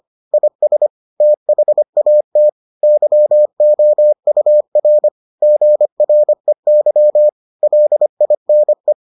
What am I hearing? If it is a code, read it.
Morse code, heard as a keyed tone, about 25 wpm, 600 Hz. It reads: IS THAT YOUR GREY LINE